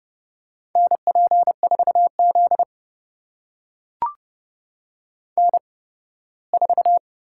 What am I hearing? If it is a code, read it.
Morse code, heard as a keyed tone, about 32 wpm, spaced as for 18 wpm, 700 Hz. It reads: NP4Z N4
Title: USA Call Signs - 4 Characters - 8x Character Spacing - 30wpm